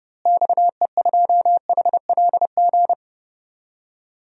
Morse code, heard as a keyed tone, about 30 wpm, 700 Hz. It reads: XE2HLG
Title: DXCC-100 Call Signs - 6 Characters - Repeated 2x - 30wpm